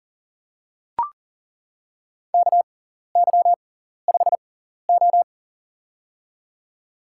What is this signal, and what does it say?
Morse code, heard as a keyed tone, about 40 wpm, spaced as for 14 wpm, 700 Hz. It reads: KY5O
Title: USA Call Signs - 4 Characters - 6x Character Spacing - 40wpm